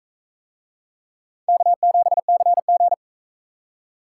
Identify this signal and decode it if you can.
Morse code, heard as a keyed tone, about 42 wpm, 700 Hz. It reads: K7CG